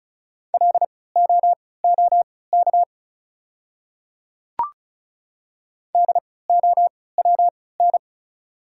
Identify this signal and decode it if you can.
Morse code, heard as a keyed tone, about 36 wpm, spaced as for 20 wpm, 690 Hz. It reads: LOOK DOWN
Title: 3x Character Spacing - Top 100 Words - 35wpm